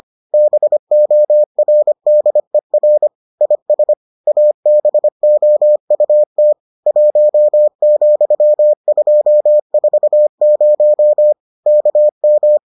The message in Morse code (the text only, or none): BORDER IS ABOUT 1,240 KM